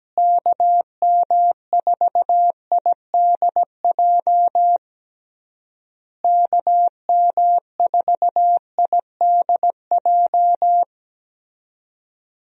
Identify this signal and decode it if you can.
Morse code, heard as a keyed tone, about 17 wpm, 700 Hz. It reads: KM4IDJ KM4IDJ
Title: USA Call Signs - 6 Characters - Repeated 3x - 17wpm